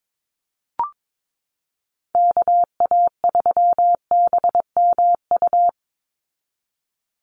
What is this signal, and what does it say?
Morse code, heard as a keyed tone, about 22 wpm, 700 Hz. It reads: KA3BMU
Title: Call Signs Repeated 2x - 22wpm